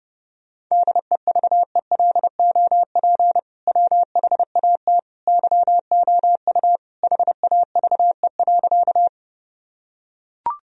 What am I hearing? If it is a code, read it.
Morse code, heard as a keyed tone, about 30 wpm, 700 Hz. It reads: DEVELOP WHAT YOU HAVE.
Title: Sentences from Top 1000 Words - Review Part 2 - 30wpm